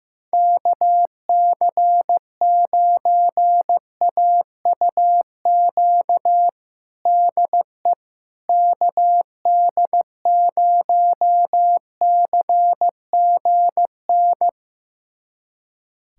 Morse code, heard as a keyed tone, about 16 wpm, 700 Hz. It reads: KC9AUQ DE KD0CGN
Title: USA Callsign DE Callsign - 6 Characters - 15wpm